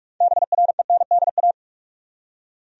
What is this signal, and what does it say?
Morse code, heard as a keyed tone, about 41 wpm, 700 Hz. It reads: BRENDA
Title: USA Common Names - 45wpm